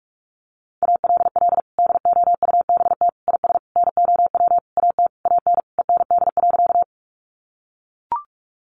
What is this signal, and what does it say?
Morse code, heard as a keyed tone, about 44 wpm, 700 Hz. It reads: ALL DOUBT IS NOW AT AN END.
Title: Sentences from Top 1500 Words 45wpm